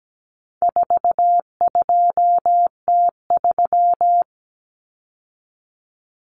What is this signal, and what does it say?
Morse code, heard as a keyed tone, about 17 wpm, 700 Hz. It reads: 42T3
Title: Common Cut Numbers - 4-Digit 17wpm